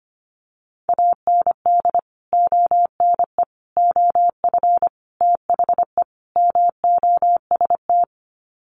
Morse code, heard as a keyed tone, about 25 wpm, 700 Hz. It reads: AND ONE OF THE MOST
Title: Sets of 5 Words Encore 25wpm